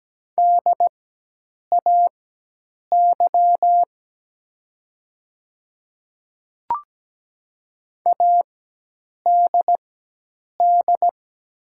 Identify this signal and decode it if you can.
Morse code, heard as a keyed tone, about 17 wpm, spaced as for 8 wpm, 700 Hz. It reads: DAY ADD
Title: 4x Character Spacing - Top 300 Words - 17wpm